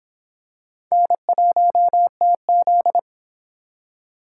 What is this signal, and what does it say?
Morse code, heard as a keyed tone, about 25 wpm, 700 Hz. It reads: N1TZ